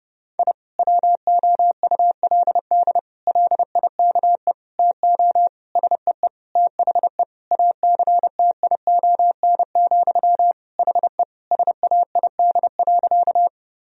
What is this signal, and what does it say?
Morse code, heard as a keyed tone, about 32 wpm, 700 Hz. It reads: I WOULD LIKE TO SEE THE ACTION, HE SAID.
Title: Sentences from Top 1000 Words - Review Part 4 - 30wpm